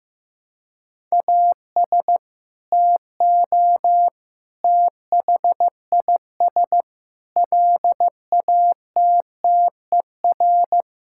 Morse code, heard as a keyed tone, about 15 wpm, 700 Hz. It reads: AS TO THIS LATTER
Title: Sentences from Top 1600 Words 15wpm